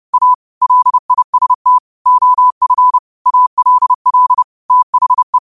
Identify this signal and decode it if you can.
Morse code, heard as a keyed tone, about 30 wpm, 1000 Hz. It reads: A LIST OF ALL THE